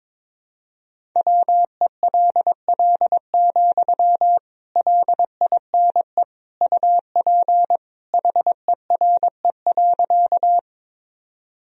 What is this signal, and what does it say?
Morse code, heard as a keyed tone, about 22 wpm, 700 Hz. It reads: WELL, LINE UP HERE.
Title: Sentences from Top 1000 Words - Review Part 1 - 22wpm